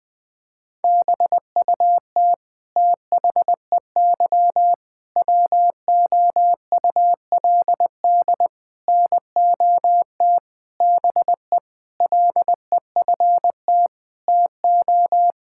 Morse code, heard as a keyed tone, about 20 wpm, 700 Hz. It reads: BUT THEY WOULD NOT BE LEFT TO
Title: Sentences from Top 1000 Words - Review Part 5 - 20wpm